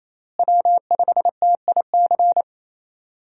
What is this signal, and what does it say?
Morse code, heard as a keyed tone, about 28 wpm, 700 Hz. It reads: W5TIC
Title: USA Call Signs - 5 Characters - Repeated 2x - 28wpm